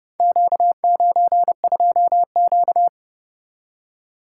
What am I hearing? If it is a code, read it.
Morse code, heard as a keyed tone, about 30 wpm, 700 Hz. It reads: Q92Q